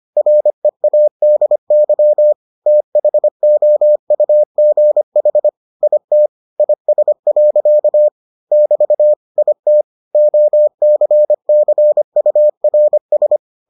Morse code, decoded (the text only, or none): READY THOUGH IT IS. <BT> IT OCCURS